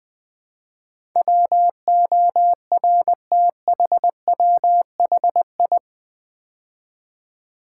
Morse code, heard as a keyed tone, about 20 wpm, 700 Hz. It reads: WORTHWHI
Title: Build A Word - 4 Letters At A Time - 10-Letter Words - 20wpm